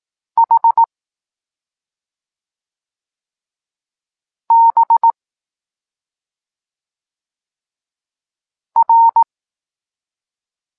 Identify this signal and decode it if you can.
Morse code, heard as a keyed tone, about 18 wpm, 920 Hz. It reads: H B R
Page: Tekst 23.3 – Ciąg dalszy doskonalenia odbioru poznanych liter